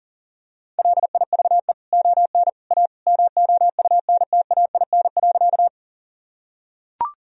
Morse code, heard as a keyed tone, about 40 wpm, 700 Hz. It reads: LIVE ON A MOUNTAIN.
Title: Sentences from Top 1000 Words - Review Part 1 - 40wpm